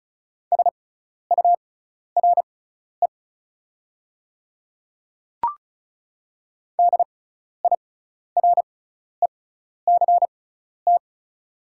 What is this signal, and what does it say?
Morse code, heard as a keyed tone, about 36 wpm, spaced as for 12 wpm, 690 Hz. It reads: SURE DIRECT
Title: 6x Character Spacing - Top 500 Words - 35wpm